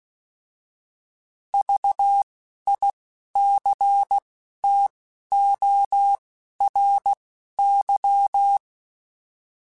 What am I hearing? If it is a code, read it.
Morse code, heard as a keyed tone, about 16 wpm, spaced as for 11 wpm, 800 Hz. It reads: VICTORY